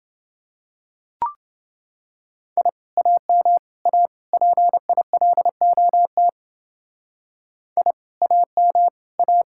Morse code, heard as a keyed tone, about 28 wpm, 700 Hz. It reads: I AM A PILOT I AM A